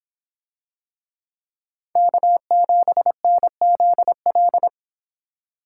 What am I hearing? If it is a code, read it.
Morse code, heard as a keyed tone, about 24 wpm, 700 Hz. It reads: K7NZL